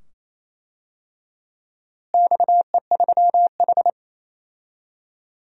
Morse code, heard as a keyed tone, about 28 wpm, 700 Hz. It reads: XE3H
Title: DXCC-100 Call Signs - 4 Characters - Repeated 2x - 28wpm